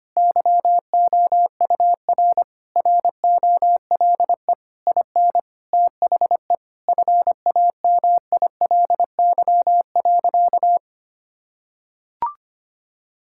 Morse code, heard as a keyed tone, about 25 wpm, 700 Hz. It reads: YOUR ROLE IN THE FAMILY.
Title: Sentences from Top 1000 Words - Review Part 6 - 25wpm